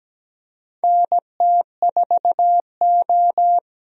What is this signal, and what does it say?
Morse code, heard as a keyed tone, about 17 wpm, 700 Hz. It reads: NT4O
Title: USA Call Signs - 4 Characters - Repeated 2x - 17wpm